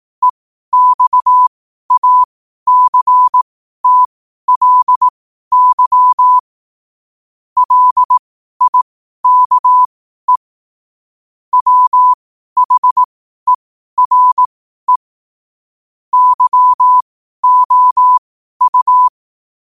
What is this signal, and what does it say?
Morse code, heard as a keyed tone, about 18 wpm, spaced as for 12 wpm, 1000 Hz. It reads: EXACTLY LIKE WHERE YOU
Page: Quotes for Thu, 14 Aug 2025 in Morse Code at 12 words per minute.